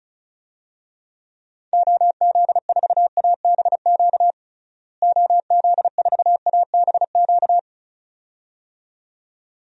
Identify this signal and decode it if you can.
Morse code, heard as a keyed tone, about 36 wpm, 700 Hz. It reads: OZ4ABQ OZ4ABQ
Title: DXCC-100 Call Signs - 6 Characters - Repeated 2x - 35wpm